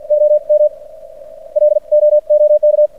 Morse code, heard as a keyed tone, about 56 wpm, spaced as for 44 wpm, 600 Hz. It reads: OM LOOK
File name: archived / CRSnet